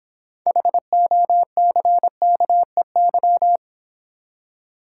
Morse code, heard as a keyed tone, about 25 wpm, 700 Hz. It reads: HOCKEY
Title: Easy Words - Speed-Racing with Hints - 20wpm